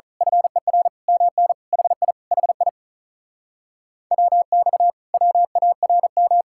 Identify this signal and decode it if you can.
Morse code, heard as a keyed tone, about 33 wpm, 700 Hz. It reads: FER MN HI HI WX WARM
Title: Speed-Racing Common QSO Phrases - 35wpm